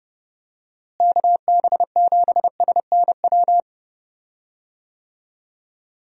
Morse code, heard as a keyed tone, about 31 wpm, 700 Hz. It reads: KB7SNW
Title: USA Call Signs - 6 Characters - Repeated 3x - 30wpm